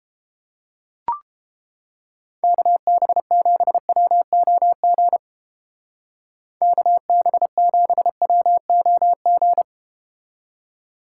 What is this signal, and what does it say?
Morse code, heard as a keyed tone, about 31 wpm, 700 Hz. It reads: KB7WOG KB7WOG